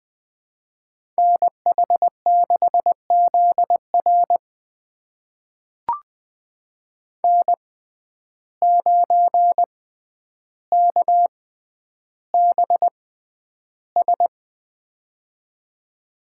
Morse code, heard as a keyed tone, about 20 wpm, spaced as for 10 wpm, 700 Hz. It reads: NH6ZR N9KBS
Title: USA Call Signs - 5 Characters - 6x Character Spacing - 20wpm